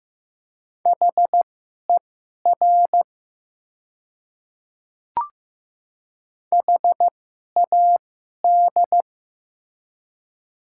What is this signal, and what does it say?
Morse code, heard as a keyed tone, about 15 wpm, spaced as for 11 wpm, 700 Hz. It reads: HER HAD
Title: 2x Character Spacing - Top 100 Words - 15wpm